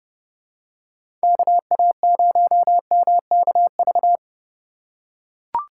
Morse code, heard as a keyed tone, about 30 wpm, 700 Hz. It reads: KA0MKV
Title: USA Call Signs - 6 Characters - 3x Character Spacing - 30wpm